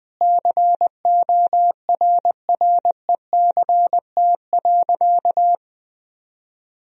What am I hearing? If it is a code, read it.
Morse code, heard as a keyed tone, about 20 wpm, 700 Hz. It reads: CORRECT.
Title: Sentences from Top 1000 Words - Review Part 2 - 20wpm